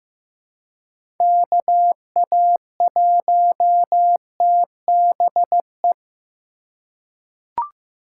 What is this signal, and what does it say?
Morse code, heard as a keyed tone, about 16 wpm, 700 Hz. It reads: KA1TBE
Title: USA Call Signs - 6 Characters - 3x Character Spacing - 15wpm